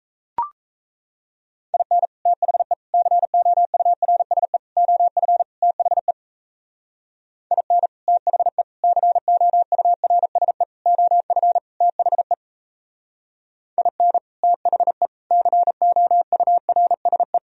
Morse code, heard as a keyed tone, about 38 wpm, 700 Hz. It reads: IN THE COURSE OF THE IN THE COURSE OF THE IN THE COURSE